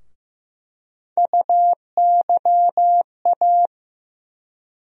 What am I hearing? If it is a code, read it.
Morse code, heard as a keyed tone, about 15 wpm, 700 Hz. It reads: UYA
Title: USA Call Sign Postfixes - 3 Letters 15wpm